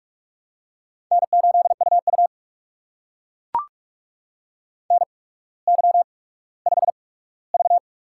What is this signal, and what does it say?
Morse code, heard as a keyed tone, about 44 wpm, spaced as for 19 wpm, 700 Hz. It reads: N8UU NY5V
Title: USA Call Signs - 4 Characters - 8x Character Spacing - 45wpm